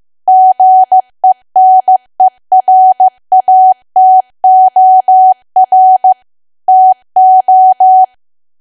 Morse code, heard as a keyed tone, about 15 wpm, 750 Hz. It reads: GENERATOR TO
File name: cwa / W1AW-Level2 / 15WPM